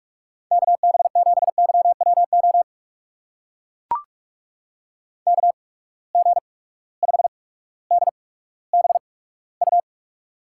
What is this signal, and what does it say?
Morse code, heard as a keyed tone, about 42 wpm, spaced as for 19 wpm, 700 Hz. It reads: KB7YWO KG5DBU
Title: USA Call Signs - 6 Characters - 8x Character Spacing - 45wpm